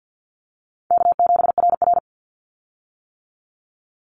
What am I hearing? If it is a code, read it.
Morse code, heard as a keyed tone, about 50 wpm, 700 Hz. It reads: K7RR
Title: Easy Call Signs - 50wpm